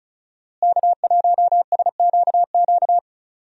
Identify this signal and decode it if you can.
Morse code, heard as a keyed tone, about 36 wpm, 700 Hz. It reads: K1SQQ
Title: USA Call Signs - 5 Characters - Repeated 3x - 35wpm